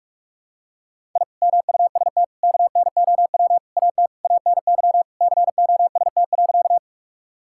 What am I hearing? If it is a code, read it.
Morse code, heard as a keyed tone, about 41 wpm, 700 Hz. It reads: I MUST KNOW AT ANY COST.
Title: Sentences from Top 1000 Words - Review Part 5 - 45wpm